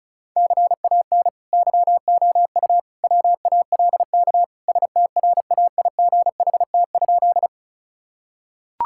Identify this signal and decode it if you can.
Morse code, heard as a keyed tone, about 36 wpm, 690 Hz. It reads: CAN YOU WALK STRAIGHT?
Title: Sentences from Top 1000 Words - Review Part 6 - 35wpm